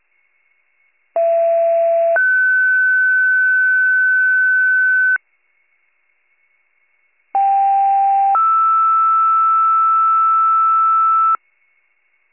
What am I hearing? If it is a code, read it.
Two-tone page, heard as a one-second tone, then a three-second tone, about 670 then 1500 Hz, about 790 then 1320 Hz. It is heard from 1.2 to 11.4 s.